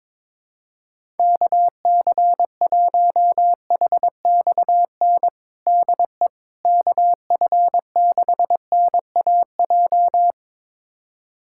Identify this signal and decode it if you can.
Morse code, heard as a keyed tone, about 22 wpm, 700 Hz. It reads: KC1HXN DE KF6NAJ
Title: USA Callsign DE Callsign - 6 Characters - 22wpm